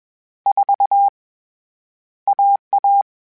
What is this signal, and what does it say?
Morse code, heard as a keyed tone, about 21 wpm, 800 Hz. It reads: 4 AA
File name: CW Code Groups-mp3s / Sets mp3s / AUV AUV4 sp @20